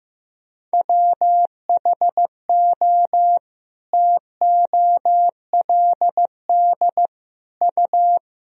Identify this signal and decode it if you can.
Morse code, heard as a keyed tone, about 16 wpm, 700 Hz. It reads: WHO TOLD U
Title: Sentences from Top 400 Words and Abbreviations 15wpm